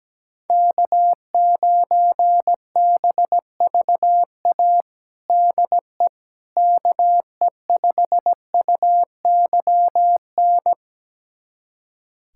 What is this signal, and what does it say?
Morse code, heard as a keyed tone, about 17 wpm, 700 Hz. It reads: K9BVA DE KE5UYN
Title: USA Callsign DE Callsign 17wpm